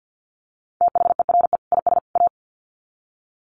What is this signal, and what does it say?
Morse code, heard as a keyed tone, about 50 wpm, 700 Hz. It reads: THERE IS A
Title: Sets of 3 Words 50wpm